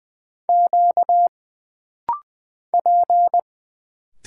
Morse code, heard as a keyed tone, about 20 wpm, 700 Hz. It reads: Q P
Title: Single Letters - Rapid-Fire 20wpm